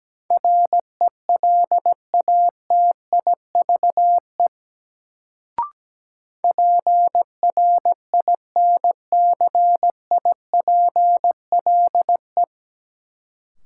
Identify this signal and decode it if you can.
Morse code, heard as a keyed tone, about 17 wpm, 700 Hz. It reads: RELATIVE PRINCIPLE
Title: Sentences from Top 1400 Words - New Word Drill Alt - 17wpm